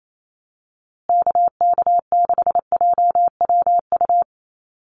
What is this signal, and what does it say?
Morse code, heard as a keyed tone, about 28 wpm, 700 Hz. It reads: KK6JWU
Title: USA Call Signs Repeated 3x 28wpm